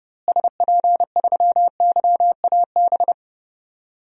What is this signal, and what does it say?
Morse code, heard as a keyed tone, about 31 wpm, 700 Hz. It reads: SP3YAB
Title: DXCC-100 Call Signs - 6 Characters - Repeated 2x - 30wpm